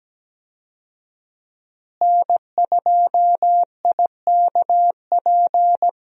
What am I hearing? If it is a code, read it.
Morse code, heard as a keyed tone, about 17 wpm, 700 Hz. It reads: N2IKP
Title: USA Call Signs - 5 Characters - Repeated 3x - 17wpm